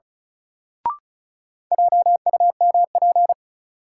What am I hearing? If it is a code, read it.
Morse code, heard as a keyed tone, about 35 wpm, 700 Hz. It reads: JUMP
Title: Sentences from Top 1300 Words - New Word Drill Alt - 35wpm